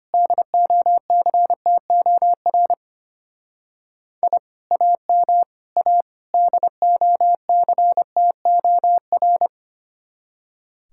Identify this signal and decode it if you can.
Morse code, heard as a keyed tone, about 29 wpm, 700 Hz. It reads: DOCTOR I AM A DOCTOR
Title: Speed-Racing Common QSO Phrases Encore - 25wpm